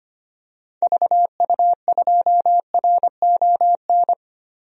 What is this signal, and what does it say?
Morse code, heard as a keyed tone, about 25 wpm, 700 Hz. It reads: VU2RON
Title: DXCC-100 Call Signs - 6 Characters - 25wpm